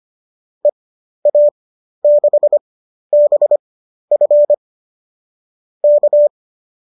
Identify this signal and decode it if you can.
Morse code, heard as a keyed tone, about 25 wpm, spaced as for 12 wpm, 590 Hz. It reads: EA6BF K
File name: qso009 12